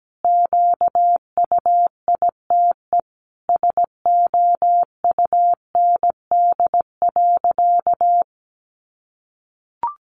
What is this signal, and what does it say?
Morse code, heard as a keyed tone, about 17 wpm, 700 Hz. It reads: QUITE SOUND.
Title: Sentences from Top 1200 Words 17wpm